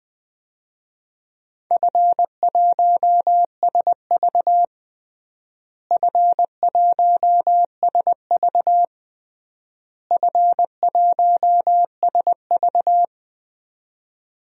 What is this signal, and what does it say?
Morse code, heard as a keyed tone, about 21 wpm, 700 Hz. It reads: F1SV F1SV F1SV
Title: DXCC-100 Call Signs - 4 Characters - Repeated 3x - 20wpm